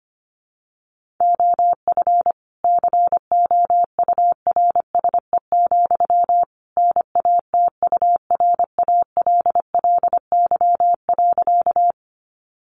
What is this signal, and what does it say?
Morse code, heard as a keyed tone, about 25 wpm, 700 Hz. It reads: OF COURSE, NATURALLY.
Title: Sentences from Top 1300 Words 25wpm